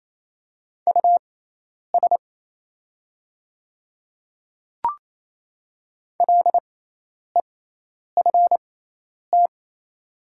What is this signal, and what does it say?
Morse code, heard as a keyed tone, about 28 wpm, spaced as for 10 wpm, 690 Hz. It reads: US LEFT
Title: 6x Character Spacing - Top 300 Words - 28wpm